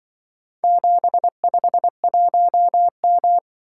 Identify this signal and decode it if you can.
Morse code, heard as a keyed tone, about 25 wpm, 700 Hz. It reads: 751M